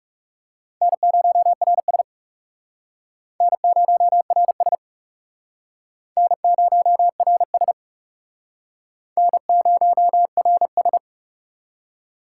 Morse code, heard as a keyed tone, about 39 wpm, 700 Hz. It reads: N0RS N0RS N0RS N0RS